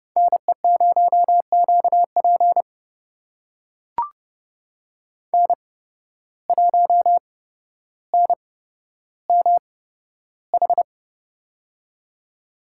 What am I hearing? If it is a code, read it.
Morse code, heard as a keyed tone, about 30 wpm, spaced as for 13 wpm, 700 Hz. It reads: NE0QP N1NMH
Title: USA Call Signs - 5 Characters - 8x Character Spacing - 30wpm